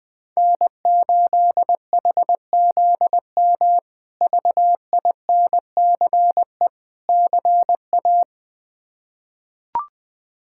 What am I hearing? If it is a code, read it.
Morse code, heard as a keyed tone, about 21 wpm, 700 Hz. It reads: N8HZM VINCE CA
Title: K1USN SST QSO Practice - 20wpm